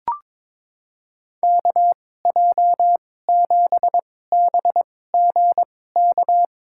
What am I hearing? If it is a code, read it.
Morse code, heard as a keyed tone, about 22 wpm, spaced as for 16 wpm, 700 Hz. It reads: KJ7BGK
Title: USA Call Signs - 6 Characters - 2x Character Spacing - 22wpm